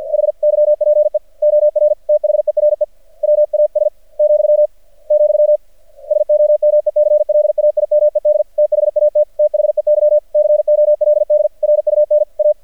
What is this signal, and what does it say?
Morse code, heard as a keyed tone, about 53 wpm, 600 Hz. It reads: H YPE OR THERE WAS , , SOMECLAIMED THAT THEY COLD RUN A
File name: archived / CRSnet